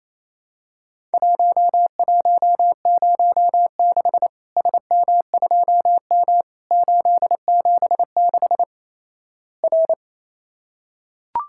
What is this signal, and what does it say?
Morse code, heard as a keyed tone, about 28 wpm, 700 Hz. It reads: 1106 SM2M 876 R
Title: Worked All Europe DX Contest - QTC Exchange - 28wpm